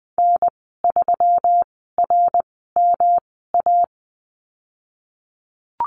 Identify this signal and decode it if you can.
Morse code, heard as a keyed tone, about 20 wpm, spaced as for 15 wpm, 700 Hz. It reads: N3RMA
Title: 128 Lesson 21g - Number 6 - Call Signs 20wpm